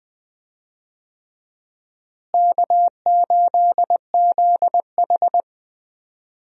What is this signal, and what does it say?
Morse code, heard as a keyed tone, about 20 wpm, 700 Hz. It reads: K8ZH
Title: USA Call Signs - 4 Characters - Repeated 3x - 20wpm